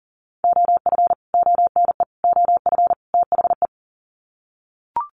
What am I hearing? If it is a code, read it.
Morse code, heard as a keyed tone, about 40 wpm, 700 Hz. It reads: OF ONE OF THE
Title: Sets of 4 Words 40wpm